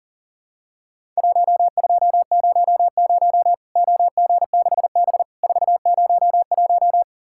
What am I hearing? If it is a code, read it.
Morse code, heard as a keyed tone, about 40 wpm, 700 Hz. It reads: 1200 OG6B 401
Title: Worked All Europe DX Contest - QTC Exchange - 40wpm